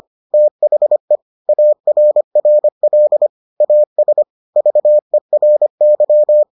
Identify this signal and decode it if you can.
Morse code, heard as a keyed tone, about 25 wpm, 600 Hz. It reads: THE ARRL AS VERY